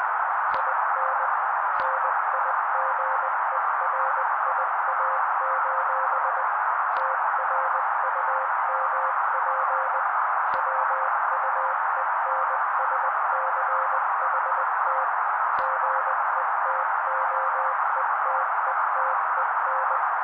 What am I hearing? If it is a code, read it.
Morse code, heard as a keyed tone, about 20 wpm, 530 Hz. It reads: IN NIGERIA: TRUMP WUENSCHT GETOETETEN
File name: daily / 18wpm noise